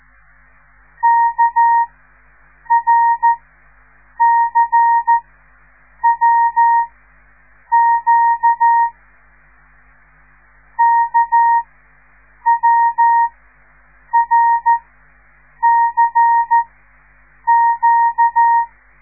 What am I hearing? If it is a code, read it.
Morse code, heard as a keyed tone, about 13 wpm, spaced as for 7 wpm, 930 Hz. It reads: KRCWQ KWRCQ
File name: radiomorningstar / morse / les 5